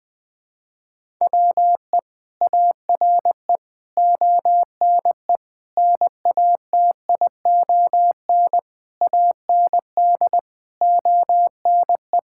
Morse code, heard as a keyed tone, about 20 wpm, 700 Hz. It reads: WE ARE ONE NATION AND ONE
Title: Sentences from Top 1000 Words - Review Part 6 - 20wpm